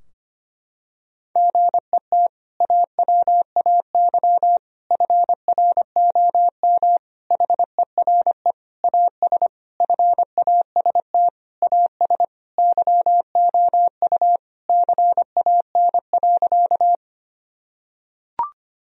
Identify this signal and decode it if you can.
Morse code, heard as a keyed tone, about 25 wpm, 700 Hz. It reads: GET AWAY FROM HERE AS FAST AS YOU CAN.
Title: Sentences from Top 1000 Words - Review Part 2 - 25wpm